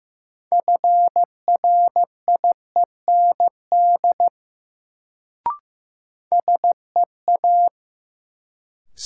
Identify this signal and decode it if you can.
Morse code, heard as a keyed tone, about 15 wpm, 700 Hz. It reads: FRIEND SEA
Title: Sentences from Top 300 Words - New Word Drill - 15wpm